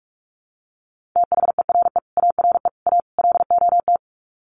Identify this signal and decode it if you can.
Morse code, heard as a keyed tone, about 44 wpm, 700 Hz. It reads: THERE ARE A LOT
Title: Sets of 4 Words 45wpm